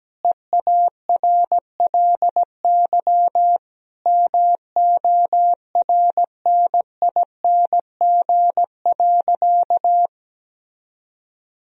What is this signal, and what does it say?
Morse code, heard as a keyed tone, about 17 wpm, 700 Hz. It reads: EARLY MORNING.
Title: Sentences from Top 1000 Words - Review Part 3 - 17wpm